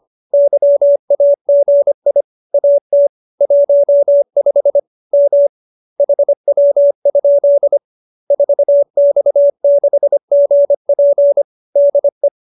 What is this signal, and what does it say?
Morse code, heard as a keyed tone, about 25 wpm, 570 Hz. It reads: YAGI AT 15 M HW? 4X6GP DE
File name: QSO310 25